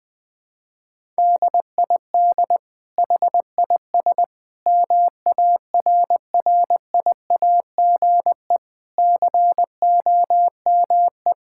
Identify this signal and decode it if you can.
Morse code, heard as a keyed tone, about 21 wpm, 700 Hz. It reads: DID HIS MARRIAGE COME
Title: Sentences from Top 1000 Words - Review Part 5 - 20wpm